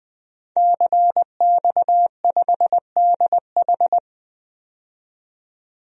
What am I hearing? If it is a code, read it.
Morse code, heard as a keyed tone, about 20 wpm, 700 Hz. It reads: CX5DH
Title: DXCC-100 Call Signs - 5 Characters - 20wpm